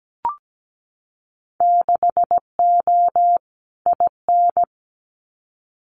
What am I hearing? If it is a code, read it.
Morse code, heard as a keyed tone, about 17 wpm, 700 Hz. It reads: 6O IN
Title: Winter Field Day Exchanges Encore 17wpm